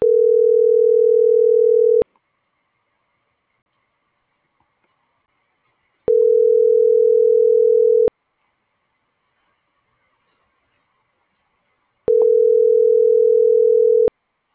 Ringback tone (North America), 440 + 480 Hz.